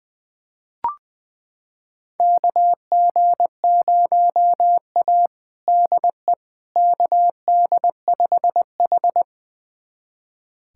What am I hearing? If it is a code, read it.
Morse code, heard as a keyed tone, about 20 wpm, 700 Hz. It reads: KG0A DE KD5H
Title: USA Callsign DE Callsign - 4 Characters - 20wpm